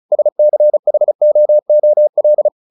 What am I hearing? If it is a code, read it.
Morse code, heard as a keyed tone, about 34 wpm, 600 Hz. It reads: SCHOOL
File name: the old man and the sea / the old man and the sea 1 35wpm